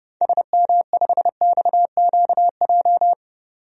Morse code, heard as a keyed tone, about 31 wpm, 700 Hz. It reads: SM5XQJ
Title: DXCC-100 Call Signs - 6 Characters - 30wpm